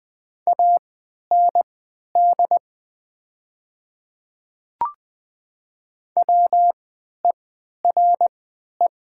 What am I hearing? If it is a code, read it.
Morse code, heard as a keyed tone, about 20 wpm, spaced as for 11 wpm, 700 Hz. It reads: AND WERE
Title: 3x Character Spacing - Top 100 Words - 20wpm